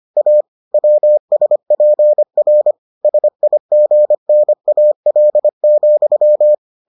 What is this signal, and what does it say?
Morse code, heard as a keyed tone, about 25 wpm, 600 Hz. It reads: A WSPR SIGNAL,